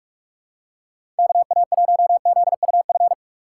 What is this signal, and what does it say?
Morse code, heard as a keyed tone, about 46 wpm, 700 Hz. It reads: KA1ZUF
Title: USA Call Signs - 6 Characters - 4x Character Spacing - 45wpm